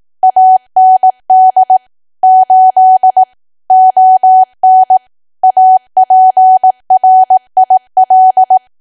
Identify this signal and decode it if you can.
Morse code, heard as a keyed tone, about 18 wpm, 750 Hz. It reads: AND 8 ON APRIL
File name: cwa / W1AW-Level2 / 18WPM